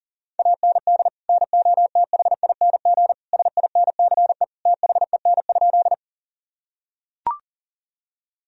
Morse code, heard as a keyed tone, about 40 wpm, 700 Hz. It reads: AND NOTHING SINCE THEN?
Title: Sentences from Top 1000 Words - Review Part 1 - 40wpm